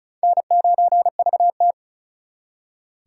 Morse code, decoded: N9VT

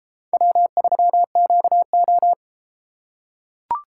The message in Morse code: W3QO

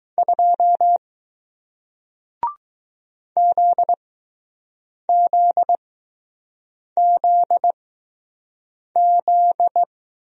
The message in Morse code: 2 ZZZZ

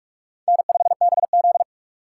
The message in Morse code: N5BZ

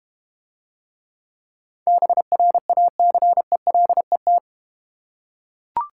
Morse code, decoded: BRACELET

